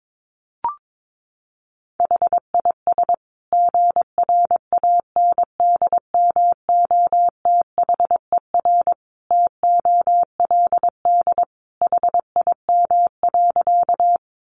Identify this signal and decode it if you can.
Morse code, heard as a keyed tone, about 22 wpm, 700 Hz. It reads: HIS GRANDMOTHER TOLD HIM.
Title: Sentences from Top 1200 Words 22wpm